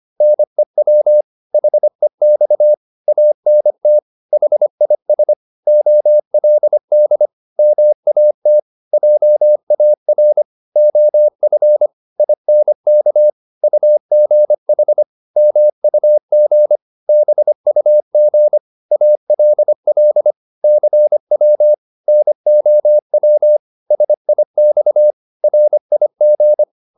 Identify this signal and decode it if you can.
Morse code, heard as a keyed tone, about 25 wpm, 600 Hz. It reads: NEW HEX ANT HIS OLD MAT JAR OF INK UGH MUG BUG ALL CW NOW SIX RIG